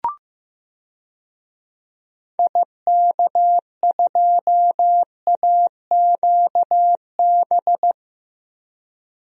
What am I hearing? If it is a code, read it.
Morse code, heard as a keyed tone, about 15 wpm, 700 Hz. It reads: IK2AQB
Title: DXCC-100 Call Signs - 6 Characters - 15wpm